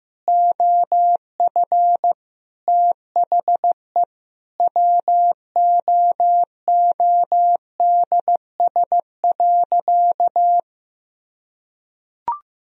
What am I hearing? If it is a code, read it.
Morse code, heard as a keyed tone, about 15 wpm, 700 Hz. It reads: OF THE WOODS.
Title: Sentences from Top 1000 Words - Review Part 2 - 15wpm